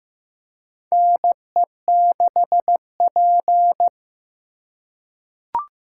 Morse code, heard as a keyed tone, about 15 wpm, 700 Hz. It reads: NE6P
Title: USA Call Signs - 4 Characters - 8x Character Spacing - 15wpm